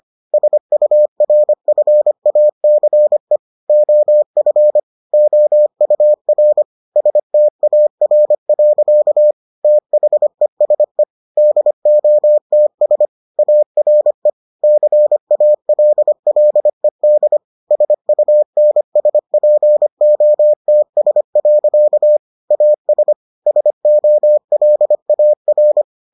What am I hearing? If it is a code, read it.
Morse code, heard as a keyed tone, about 25 wpm, 600 Hz. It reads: SURFACE OF OUR STAR. THESE DOTS ARE CALLED SUNSPOTS. AS SOLAR